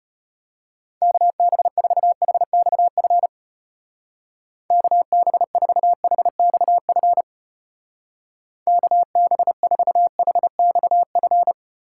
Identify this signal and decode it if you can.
Morse code, heard as a keyed tone, about 31 wpm, 700 Hz. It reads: KB4HXF KB4HXF KB4HXF